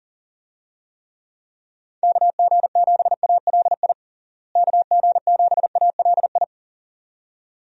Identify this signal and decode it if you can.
Morse code, heard as a keyed tone, about 40 wpm, 700 Hz. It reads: KG7ALI KG7ALI
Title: USA Call Signs - 6 Characters - Repeated 2x - 40wpm